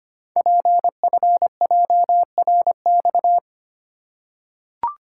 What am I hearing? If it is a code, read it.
Morse code, heard as a keyed tone, about 25 wpm, 690 Hz. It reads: PFJRX
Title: Code Groups - 5 Letters - 25wpm